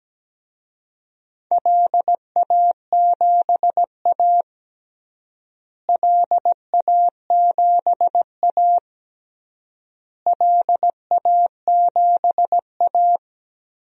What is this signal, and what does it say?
Morse code, heard as a keyed tone, about 17 wpm, 700 Hz. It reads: LA7A LA7A LA7A
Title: DXCC-100 Call Signs - 4 Characters - Repeated 3x - 17wpm